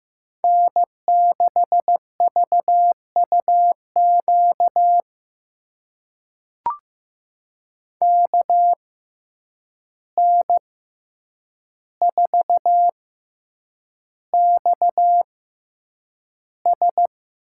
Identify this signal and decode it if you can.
Morse code, heard as a keyed tone, about 15 wpm, spaced as for 8 wpm, 700 Hz. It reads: N6VUQ KN4XS
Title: USA Call Signs - 5 Characters - 6x Character Spacing - 15wpm